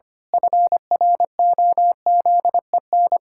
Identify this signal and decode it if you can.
Morse code, heard as a keyed tone, about 25 wpm, 700 Hz. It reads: FROZEN
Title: Pangram Challenge v2 - 25wpm